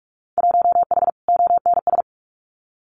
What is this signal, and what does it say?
Morse code, heard as a keyed tone, about 44 wpm, 700 Hz. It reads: 1H ONS
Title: Winter Field Day Exchanges 45wpm